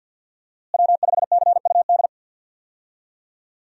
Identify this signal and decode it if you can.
Morse code, heard as a keyed tone, about 50 wpm, 700 Hz. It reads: W5CUD
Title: USA Call Signs - 5 Characters - Repeated 2x - 50wpm